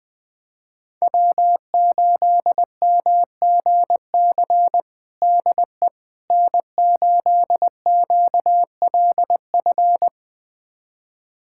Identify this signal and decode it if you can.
Morse code, heard as a keyed tone, about 20 wpm, 700 Hz. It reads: W8MGC DE N8QLF
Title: USA Callsign DE Callsign - 5 Characters - 20wpm